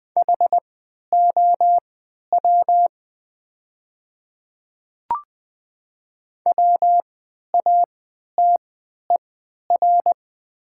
Morse code, heard as a keyed tone, about 20 wpm, spaced as for 11 wpm, 700 Hz. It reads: HOW WATER
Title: 3x Character Spacing - Top 100 Words - 20wpm